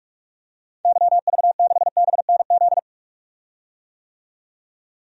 Morse code, heard as a keyed tone, about 46 wpm, 700 Hz. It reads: YV6BNZ
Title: DXCC-100 Call Signs - 6 Characters - 45wpm